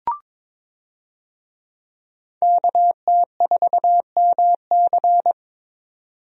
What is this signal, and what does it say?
Morse code, heard as a keyed tone, about 22 wpm, 700 Hz. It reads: KT4MC
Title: USA Call Signs - 5 Characters - 22wpm